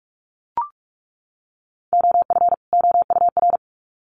Morse code, heard as a keyed tone, about 44 wpm, 700 Hz. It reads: OF OUR
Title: Sets of 2 Words 45wpm